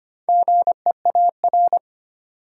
Morse code, heard as a keyed tone, about 25 wpm, 700 Hz. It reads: GEAR
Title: Sentences from Top 1400 Words - New Word Drill Alt - 25wpm